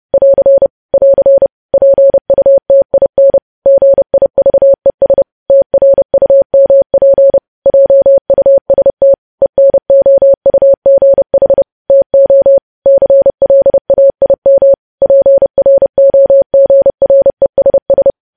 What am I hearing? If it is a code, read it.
Morse code, heard as a keyed tone, about 30 wpm, 550 Hz. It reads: <AR> <AR> PUTIN GIVES TRUMP JUST ENOUGH TO CLAIM PROGRESS
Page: news headlines in morse code at 30 wpm